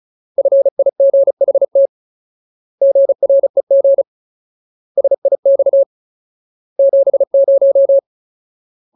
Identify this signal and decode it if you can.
Morse code, heard as a keyed tone, about 34 wpm, 550 Hz. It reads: FIGHT GREG SIX 70
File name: Ultimate Word List / 35WPM Ultimate